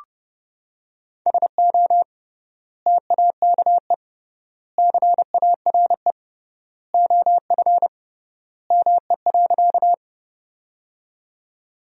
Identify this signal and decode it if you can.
Morse code, heard as a keyed tone, about 30 wpm, 700 Hz. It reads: SO TAKE CARE OF ME.
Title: 3x Word-Spacing - Sentences from Top 300 Words - 30wpm